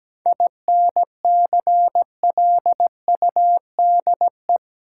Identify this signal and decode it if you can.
Morse code, heard as a keyed tone, about 17 wpm, 700 Hz. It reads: INCLUDE
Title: Sentences from Top 1000 Words - Review Part 3 - New Word Drill - 17wpm